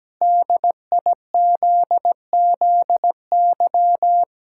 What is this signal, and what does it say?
Morse code, heard as a keyed tone, about 17 wpm, 700 Hz. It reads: DIZZY
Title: Pangram Challenge v2 - 17wpm